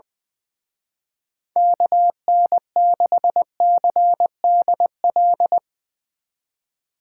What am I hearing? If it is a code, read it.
Morse code, heard as a keyed tone, about 21 wpm, 700 Hz. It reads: KN6CDL
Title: Speed Racing - USA 2x3 Call Signs 20wpm